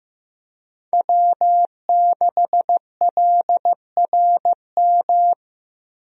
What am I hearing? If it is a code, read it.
Morse code, heard as a keyed tone, about 15 wpm, 700 Hz. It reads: W6LRM
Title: USA Call Signs - 5 Characters - Repeated 3x - 15wpm